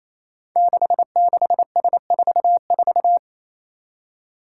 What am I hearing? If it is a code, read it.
Morse code, heard as a keyed tone, about 27 wpm, 700 Hz. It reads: 66S44
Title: Five Challenging Letter-Number Combinations 28wpm